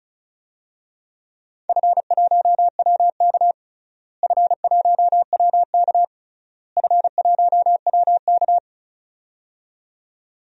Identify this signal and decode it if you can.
Morse code, heard as a keyed tone, about 36 wpm, 690 Hz. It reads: F1WK F1WK F1WK
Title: DXCC-100 Call Signs - 4 Characters - Repeated 3x - 35wpm